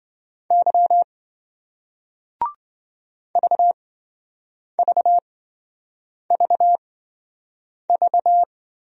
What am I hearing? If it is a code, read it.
Morse code, heard as a keyed tone, about 27 wpm, spaced as for 7 wpm, 700 Hz. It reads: Y VVVV